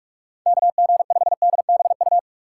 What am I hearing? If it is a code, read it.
Morse code, heard as a keyed tone, about 44 wpm, 700 Hz. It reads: KG5DBU
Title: USA Call Signs - 6 Characters - 6x Character Spacing - 45wpm